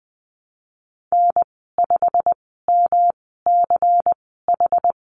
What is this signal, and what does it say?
Morse code, heard as a keyed tone, about 19 wpm, spaced as for 14 wpm, 700 Hz. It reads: N5MCH
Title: 179 Lesson 29g - Slash - Call Signs 20wpm